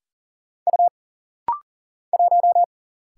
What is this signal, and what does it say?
Morse code, heard as a keyed tone, about 39 wpm, 700 Hz. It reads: U 1
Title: Single Letter-Number Alt - Rapid-Fire 40wpm